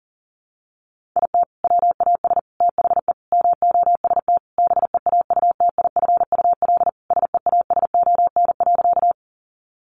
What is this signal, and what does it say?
Morse code, heard as a keyed tone, about 40 wpm, 700 Hz. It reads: IT WAS THE MOST BEAUTIFUL SEASON.
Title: Sentences from Top 1400 Words 40wpm